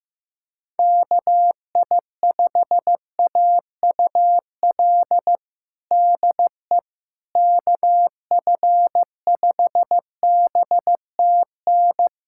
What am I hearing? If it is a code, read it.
Morse code, heard as a keyed tone, about 15 wpm, 700 Hz. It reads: KI5AUL DE KF5BTN
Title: USA Callsign DE Callsign - 6 Characters - 15wpm